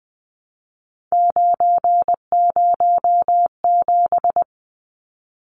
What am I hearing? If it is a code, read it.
Morse code, heard as a keyed tone, about 20 wpm, 700 Hz. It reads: 907
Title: Numbers - 3-Digit 20wpm